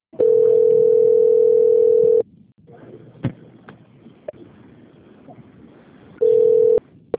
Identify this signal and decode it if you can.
Ringback tone (North America), 440 + 480 Hz, with speech.
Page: Sorocaba - SP